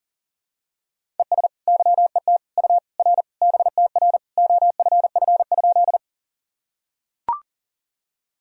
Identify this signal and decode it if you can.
Morse code, heard as a keyed tone, about 40 wpm, 700 Hz. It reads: ES YET U R BTR OFF?
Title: Sentences from Top 500 Words and Abbreviations 40wpm